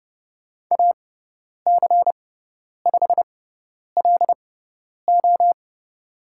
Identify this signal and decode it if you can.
Morse code, heard as a keyed tone, about 30 wpm, spaced as for 10 wpm, 700 Hz. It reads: AC5LO
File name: Lesson 16g - Letter c - Call Signs 30wpm